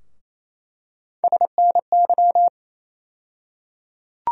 Morse code, heard as a keyed tone, about 28 wpm, 700 Hz. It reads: SNY
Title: USA Call Sign Postfixes - 3 Letters 28wpm